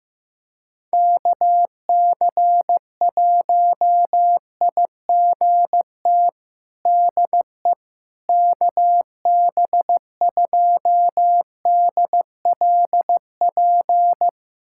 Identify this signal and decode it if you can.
Morse code, heard as a keyed tone, about 15 wpm, 700 Hz. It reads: KC1IGT DE KB2DLP
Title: USA Callsign DE Callsign - 6 Characters - 15wpm